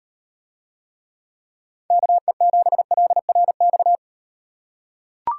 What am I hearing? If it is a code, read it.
Morse code, heard as a keyed tone, about 38 wpm, 700 Hz. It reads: KE7LRX